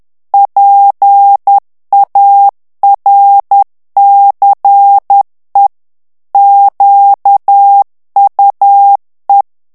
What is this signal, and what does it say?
Morse code, heard as a keyed tone, about 10 wpm, 800 Hz. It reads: PARCE QUE
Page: Français - texte en clair 1 - vitesse : 10 mots minute